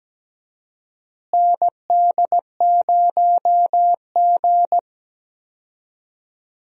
Morse code, heard as a keyed tone, about 17 wpm, 700 Hz. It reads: ND0G
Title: USA Call Signs - 4 Characters - Repeated 2x - 17wpm